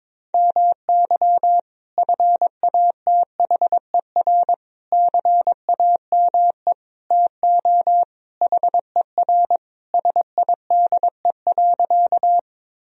Morse code, heard as a keyed tone, about 22 wpm, 700 Hz. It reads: MY FATHER CAME TO HER SIDE.
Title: Sentences from Top 1000 Words - Review Part 1 - 22wpm